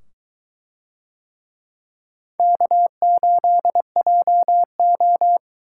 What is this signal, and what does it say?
Morse code, heard as a keyed tone, about 22 wpm, 700 Hz. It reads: K8JO